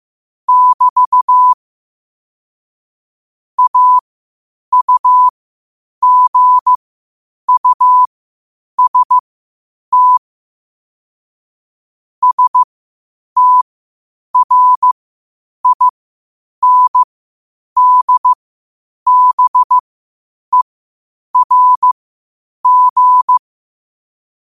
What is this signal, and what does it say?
Morse code, heard as a keyed tone, about 15 wpm, spaced as for 8 wpm, 1000 Hz. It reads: <BT> AUGUST STRINDBERG
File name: qotdfiles / 8wpm